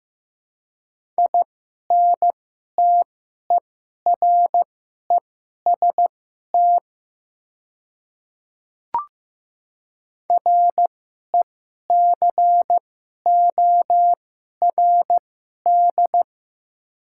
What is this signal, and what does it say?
Morse code, heard as a keyed tone, about 15 wpm, spaced as for 11 wpm, 700 Hz. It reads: INTEREST RECORD
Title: 2x Character Spacing - Top 500 Words - 15wpm